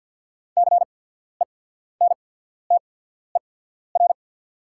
Morse code, heard as a keyed tone, about 48 wpm, spaced as for 14 wpm, 690 Hz. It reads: CENTER
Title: 8x Character Spacing - Top 500 Words - 50wpm